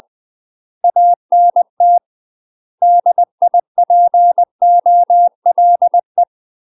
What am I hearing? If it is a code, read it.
Morse code, heard as a keyed tone, about 20 wpm, 700 Hz. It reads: ANT DIPOLE
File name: UK MASTER QSOs TRAINING / 20WPM / UK MASTER QSOs 0063